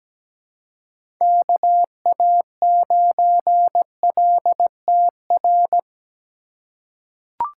Morse code, heard as a keyed tone, about 17 wpm, 700 Hz. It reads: KA9LTR
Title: USA Call Signs - 6 Characters - 2x Character Spacing - 17wpm